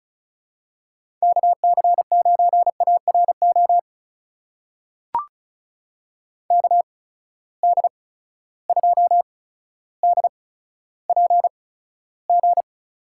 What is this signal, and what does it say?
Morse code, heard as a keyed tone, about 36 wpm, spaced as for 15 wpm, 700 Hz. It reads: KC9ARO KD2DPG
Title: USA Call Signs - 6 Characters - 8x Character Spacing - 35wpm